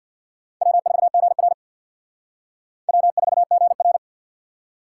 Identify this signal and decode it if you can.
Morse code, heard as a keyed tone, about 55 wpm, 700 Hz. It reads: W4GR W4GR
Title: Speed Racing - USA 1x2 Call Signs 50wpm